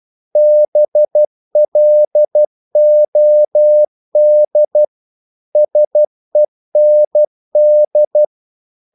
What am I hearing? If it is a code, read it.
Morse code, heard as a keyed tone, about 12 wpm, 600 Hz. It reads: BLOD SEND
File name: Korte ord DK 12wpm